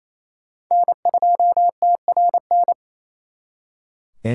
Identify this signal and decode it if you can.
Morse code, heard as a keyed tone, about 35 wpm, 700 Hz. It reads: N2TRN
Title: Speed Racing - USA 1x3 Call Signs 28wpm